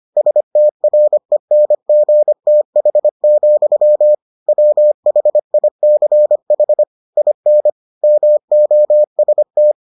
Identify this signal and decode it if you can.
Morse code, heard as a keyed tone, about 25 wpm, 600 Hz. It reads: STRENGTH, WHICH IN MOST